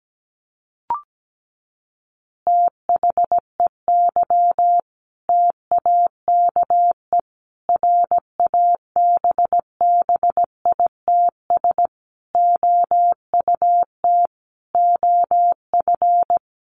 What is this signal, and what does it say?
Morse code, heard as a keyed tone, about 17 wpm, 700 Hz. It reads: THEY TAKE RABBITS OUT OF
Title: Easy Phrases 17wpm